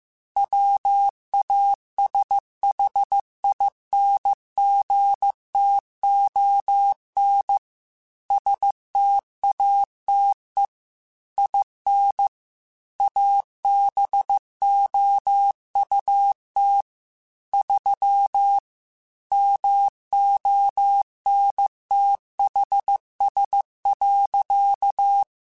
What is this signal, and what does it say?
Morse code, heard as a keyed tone, about 15 wpm, 800 Hz. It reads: WASHINGTON STATE IN ABOUT 3 MONTHS.